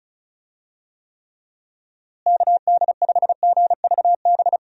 Morse code, heard as a keyed tone, about 36 wpm, 700 Hz. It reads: KD5GVB
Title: USA Call Signs - 6 Characters - Repeated 2x - 35wpm